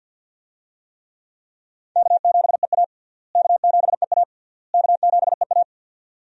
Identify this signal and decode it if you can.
Morse code, heard as a keyed tone, about 47 wpm, 700 Hz. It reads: K7EA K7EA K7EA
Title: USA Call Signs - 4 Characters - Repeated 3x - 50wpm